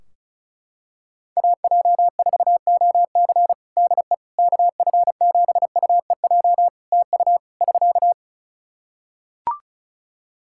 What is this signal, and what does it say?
Morse code, heard as a keyed tone, about 36 wpm, 700 Hz. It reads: AJ4OC DE KF7UEJ TU <SK>
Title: Practice QSO - Exchange 4 - Ending - 35wpm